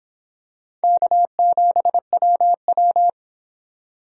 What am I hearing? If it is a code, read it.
Morse code, heard as a keyed tone, about 26 wpm, 700 Hz. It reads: K7WW